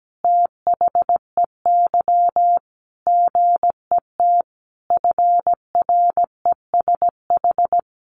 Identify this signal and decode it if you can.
Morse code, heard as a keyed tone, about 17 wpm, 700 Hz. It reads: THEY GET FRESH
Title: Sentences from Top 1200 Words 17wpm